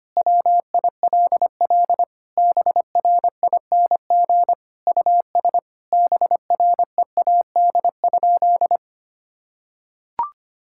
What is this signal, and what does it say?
Morse code, heard as a keyed tone, about 25 wpm, 700 Hz. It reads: WILL BRING US BREAD?
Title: Sentences from Top 1600 Words 25wpm